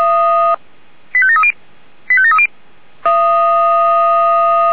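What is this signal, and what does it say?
Five-tone selective call (ZVEI), heard as quick runs of five tones, tones of about 80 ms. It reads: 85730 85730
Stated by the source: ZVEI five-tone selective call 85729 received by SDR.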